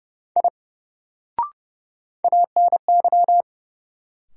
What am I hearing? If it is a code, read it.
Morse code, heard as a keyed tone, about 30 wpm, 700 Hz. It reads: I ANY
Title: Sentences from Top 200 Words - New Word Drill - 30wpm